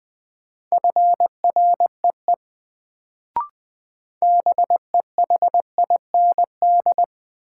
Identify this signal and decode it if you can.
Morse code, heard as a keyed tone, about 20 wpm, 700 Hz. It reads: FREE BEHIND
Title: Sentences from Top 500 Words - New Word Drill - 20wpm